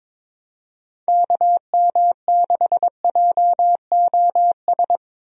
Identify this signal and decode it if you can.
Morse code, heard as a keyed tone, about 22 wpm, 700 Hz. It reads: KM6JOS
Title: USA Call Signs - 6 Characters - Repeated 3x - 22wpm